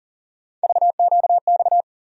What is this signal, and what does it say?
Morse code, heard as a keyed tone, about 40 wpm, 700 Hz. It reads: VQX
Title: USA Call Sign Postfixes - 3 Letters 40wpm